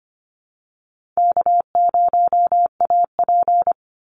Morse code, heard as a keyed tone, about 25 wpm, 700 Hz. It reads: K0AP
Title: Easy Call Signs Repeated 3x - 25wpm